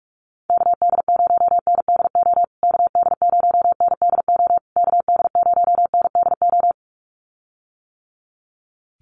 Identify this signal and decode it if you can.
Morse code, heard as a keyed tone, about 44 wpm, 700 Hz. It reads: KD0NDO KD0NDO KD0NDO
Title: USA Call Signs Repeated 3x 45wpm